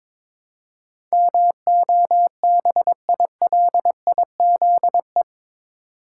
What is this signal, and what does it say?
Morse code, heard as a keyed tone, about 22 wpm, 700 Hz. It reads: MOBILIZE
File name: practice / 22wpm / Eight Letter Words - 22wpm